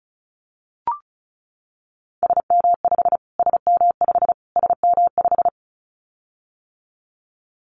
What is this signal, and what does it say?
Morse code, heard as a keyed tone, about 34 wpm, 700 Hz. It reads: SM5 SM5 SM5
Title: Callsign Prefixes for Top 25 DXCC - Repeated 3x - 35wpm